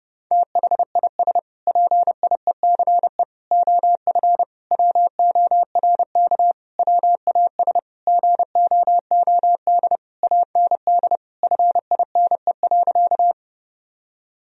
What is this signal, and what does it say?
Morse code, heard as a keyed tone, about 29 wpm, 700 Hz. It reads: THIS PIECE OF WORK WAS GOOD AND FINE.
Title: Sentences from Top 1000 Words - Review Part 2 - 30wpm